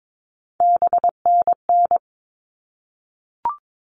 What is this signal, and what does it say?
Morse code, heard as a keyed tone, about 22 wpm, 700 Hz. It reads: BNN
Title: All Cut Numbers - 3-Digit 22wpm